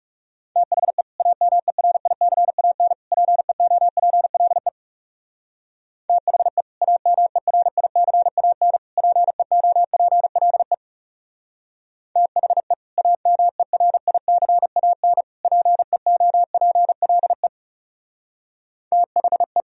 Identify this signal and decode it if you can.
Morse code, heard as a keyed tone, about 39 wpm, 700 Hz. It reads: THE AMERICAN PEOPLE THE AMERICAN PEOPLE THE AMERICAN PEOPLE THE